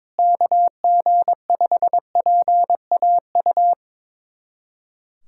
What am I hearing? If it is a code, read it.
Morse code, heard as a keyed tone, about 22 wpm, 700 Hz. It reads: KG5PAU
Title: USA Callsign DE Callsign 22wpm